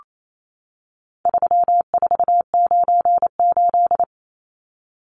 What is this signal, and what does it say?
Morse code, heard as a keyed tone, about 28 wpm, 700 Hz. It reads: 3498
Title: Numbers - 4-Digit 28wpm